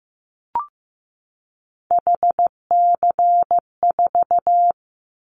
Morse code, heard as a keyed tone, about 15 wpm, 700 Hz. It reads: HC4
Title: Callsign Prefixes for Top 100 DXCC - 15wpm